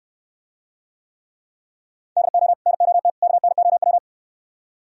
Morse code, heard as a keyed tone, about 68 wpm, 700 Hz. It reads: DO NOT KNOW